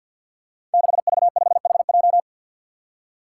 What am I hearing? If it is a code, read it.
Morse code, heard as a keyed tone, about 48 wpm, 700 Hz. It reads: 6V5HJ
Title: Five Challenging Letter-Number Combinations 50wpm